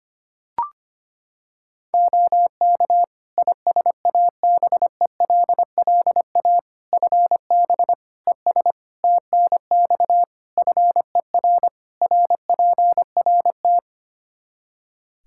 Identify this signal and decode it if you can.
Morse code, heard as a keyed tone, about 25 wpm, 700 Hz. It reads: OK ISABELLA FB ES TNX FER RPRT
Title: Practice QSO - Exchange 2 - Info - 25wpm